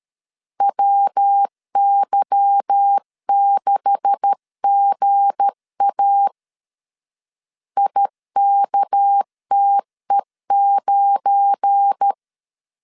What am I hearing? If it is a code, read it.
Morse code, heard as a keyed tone, about 13 wpm, 800 Hz. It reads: WY6GA IKTE9
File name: LEZIONI / 9 286 12wpm 2305